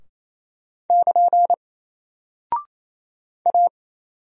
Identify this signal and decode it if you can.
Morse code, heard as a keyed tone, about 27 wpm, 700 Hz. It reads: <KN> A
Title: Single Character (Letter:Number:Punctuation:Prosign) Alt 28wpm